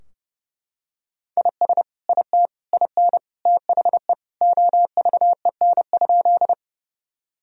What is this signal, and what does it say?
Morse code, heard as a keyed tone, about 32 wpm, 700 Hz. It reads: IS IT IN THE OVEN?
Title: Sentences from Top 1000 Words - Review Part 4 - 30wpm